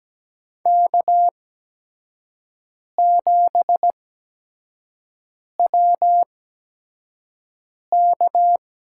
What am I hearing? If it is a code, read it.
Morse code, heard as a keyed tone, about 17 wpm, spaced as for 5 wpm, 700 Hz. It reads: K7WK
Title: USA Call Signs - 5 Characters - 8x Character Spacing - 17wpm